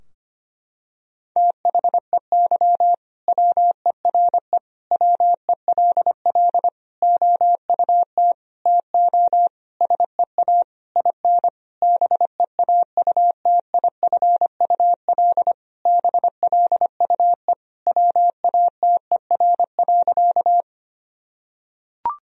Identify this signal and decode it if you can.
Morse code, heard as a keyed tone, about 25 wpm, 700 Hz. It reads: THEY WERE WELL OUT TO SEA IN BEAUTIFUL BLUE WATER.
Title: Sentences from Top 1000 Words - Review Part 3 - 25wpm